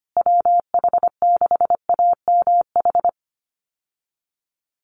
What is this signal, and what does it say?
Morse code, heard as a keyed tone, about 25 wpm, 700 Hz. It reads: WH6AMH
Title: USA Call Signs Repeated 3x 25wpm